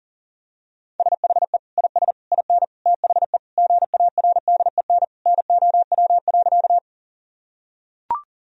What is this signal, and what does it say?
Morse code, heard as a keyed tone, about 40 wpm, 700 Hz. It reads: SHE IS IN THE GARDEN NOW.
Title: Sentences from Top 1000 Words - Review Part 4 - 40wpm